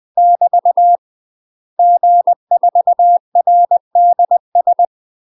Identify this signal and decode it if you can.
Morse code, heard as a keyed tone, about 20 wpm, 700 Hz. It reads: <BT> G4RDS